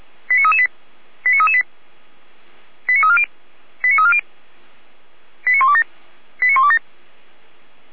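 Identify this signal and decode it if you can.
Five-tone selective call (ZVEI), heard as quick runs of five tones, tones of about 70 ms. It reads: 89338 89338 89340 89340 89127 89127
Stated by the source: ZVEI five-tone selective call received on an SDR.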